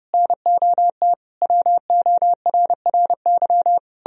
Morse code, heard as a keyed tone, about 40 wpm, spaced as for 35 wpm, 700 Hz. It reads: NOT WORRY
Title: Speed-Racing - Sets of 5 Words 30wpm